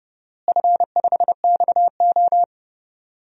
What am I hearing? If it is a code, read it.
Morse code, heard as a keyed tone, about 30 wpm, 700 Hz. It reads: F5XO
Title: DXCC-100 Call Signs - 4 Characters - Repeated 2x - 30wpm